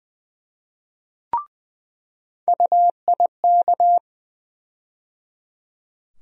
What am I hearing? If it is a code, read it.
Morse code, heard as a keyed tone, about 20 wpm, 700 Hz. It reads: UIK
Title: USA Call Sign Postfixes - 3 Letters 20wpm